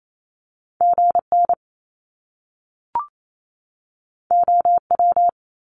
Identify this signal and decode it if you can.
Morse code, heard as a keyed tone, about 28 wpm, 700 Hz. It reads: GN OW
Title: Top Two Letter Combinations in Words 28wpm